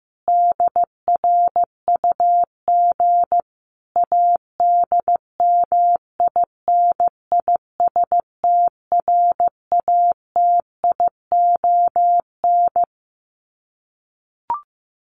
Morse code, heard as a keyed tone, about 15 wpm, 700 Hz. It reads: DRUG ADMINISTRATION
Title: Sets of 5 Words 15wpm